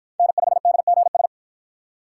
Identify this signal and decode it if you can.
Morse code, heard as a keyed tone, about 50 wpm, 700 Hz. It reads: N5DGS